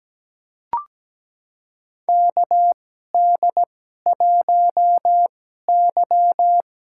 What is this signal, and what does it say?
Morse code, heard as a keyed tone, about 17 wpm, spaced as for 12 wpm, 700 Hz. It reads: KD1Y
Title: USA Call Signs - 2x Character Spacing - 17wpm